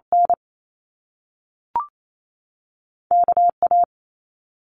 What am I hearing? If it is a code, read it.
Morse code, heard as a keyed tone, about 28 wpm, 700 Hz. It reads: N KA
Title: Top Two Letter Combinations in Words 28wpm